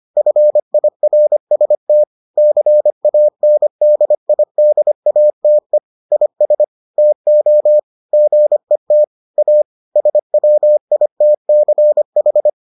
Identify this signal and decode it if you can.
Morse code, heard as a keyed tone, about 25 wpm, 600 Hz. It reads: FIRST CANDIDATE IS TO GET A SWITCH